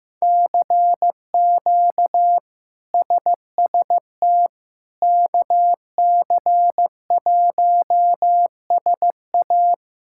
Morse code, heard as a keyed tone, about 15 wpm, 700 Hz. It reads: CQ SST KC1SA
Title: K1USN SST QSO Practice - 15wpm